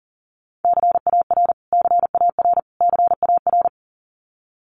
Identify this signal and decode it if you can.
Morse code, heard as a keyed tone, about 37 wpm, 700 Hz. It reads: CAR CAR CAR
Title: Top 500 Words Repeated 40wpm